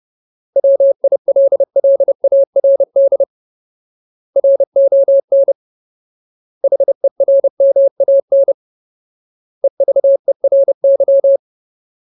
Morse code, heard as a keyed tone, about 30 wpm, 550 Hz. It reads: WILLARD RON HERMAN EVERY
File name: Ultimate Word List / 30WPM Ultimate